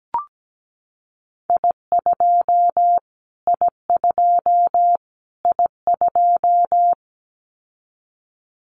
Morse code, heard as a keyed tone, about 17 wpm, 700 Hz. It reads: I2 I2 I2
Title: Callsign Prefixes for Top 25 DXCC - Repeated 3x - 17wpm